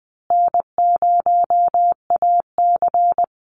Morse code, heard as a keyed tone, about 20 wpm, 700 Hz. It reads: N0AC
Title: Easy Call Signs Repeated 3x - 20wpm